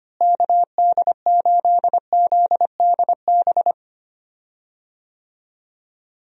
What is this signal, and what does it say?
Morse code, heard as a keyed tone, about 25 wpm, 700 Hz. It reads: KD8ZDB
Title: USA Call Signs - 6 Characters - Repeated 3x - 25wpm